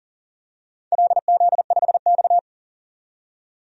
Morse code, heard as a keyed tone, about 40 wpm, 700 Hz. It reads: LZ5X
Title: DXCC-100 Call Signs - 4 Characters - 40wpm